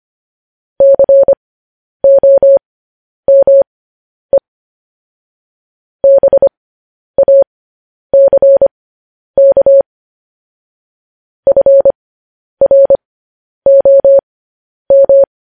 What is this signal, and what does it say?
Morse code, heard as a keyed tone, about 25 wpm, spaced as for 10 wpm, 550 Hz. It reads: COME BACK FROM
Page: news headlines in morse code at 10 wpm